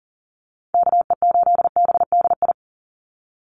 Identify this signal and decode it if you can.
Morse code, heard as a keyed tone, about 40 wpm, 700 Hz. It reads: KE8BDI
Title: USA Call Signs 40wpm